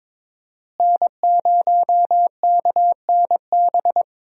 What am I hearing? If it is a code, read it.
Morse code, heard as a keyed tone, about 22 wpm, 700 Hz. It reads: N0KNB
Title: USA Call Signs - 5 Characters - Repeated 2x - 22wpm